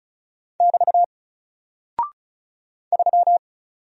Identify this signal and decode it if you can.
Morse code, heard as a keyed tone, about 36 wpm, 700 Hz. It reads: <BT> 3
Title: Single Character (Letter:Number:Punctuation:Prosign) Alt 35wpm